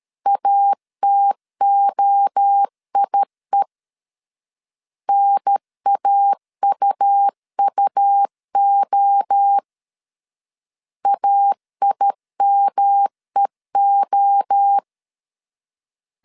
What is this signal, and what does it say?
Morse code, heard as a keyed tone, about 13 wpm, 800 Hz. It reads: ATOIE NAUUO AIMEO
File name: CW / DET / 1 3 1500C S C 12wpm 2750